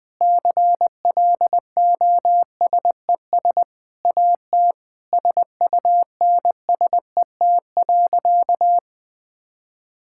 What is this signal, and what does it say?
Morse code, heard as a keyed tone, about 20 wpm, 700 Hz. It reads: CLOSES AT SUNSET.
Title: Sentences with 7-Word Phrases - 20wpm